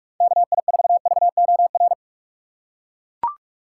Morse code, heard as a keyed tone, about 41 wpm, 700 Hz. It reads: KI4VOR
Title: USA Call Signs - 6 Characters - 4x Character Spacing - 45wpm